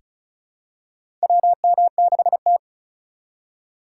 Morse code, heard as a keyed tone, about 34 wpm, 700 Hz. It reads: WM6T
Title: USA Call Signs - 4 Characters - 6x Character Spacing - 35wpm